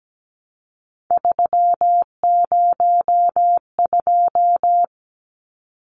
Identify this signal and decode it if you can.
Morse code, heard as a keyed tone, about 17 wpm, 700 Hz. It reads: 302
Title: Numbers - 3-Digit 17wpm